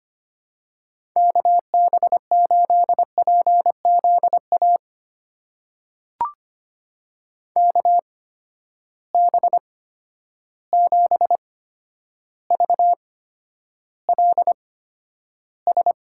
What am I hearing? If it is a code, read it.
Morse code, heard as a keyed tone, about 25 wpm, spaced as for 11 wpm, 700 Hz. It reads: KB8PZA KB7VLS
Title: USA Call Signs - 6 Characters - 8x Character Spacing - 25wpm